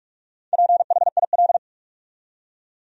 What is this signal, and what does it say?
Morse code, heard as a keyed tone, about 46 wpm, 700 Hz. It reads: PHIL
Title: USA Common Names - Max 5 Letters - 45wpm